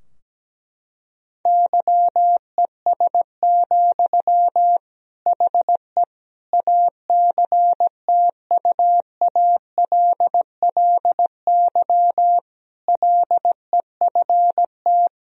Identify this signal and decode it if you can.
Morse code, heard as a keyed tone, about 17 wpm, 700 Hz. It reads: YES, HE ACTUALLY LEFT
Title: Sentences from Top 1000 Words - Review Part 4 - 17wpm